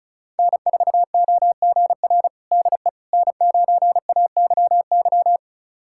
Morse code, heard as a keyed tone, about 36 wpm, 700 Hz. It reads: N4OGR DE N9AYY
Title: USA Callsign DE Callsign - 5 Characters - 35wpm